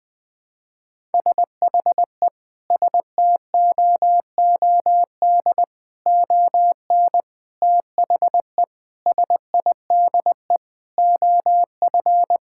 Morse code, heard as a keyed tone, about 20 wpm, 700 Hz. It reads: SHE STOOD ON THE SIDE OF
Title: Sentences with 4-Word Phrases - 20wpm